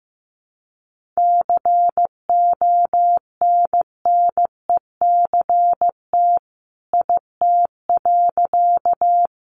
Morse code, heard as a keyed tone, about 15 wpm, 700 Hz. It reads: CONNECT IT.
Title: Sentences from Top 1500 Words 15wpm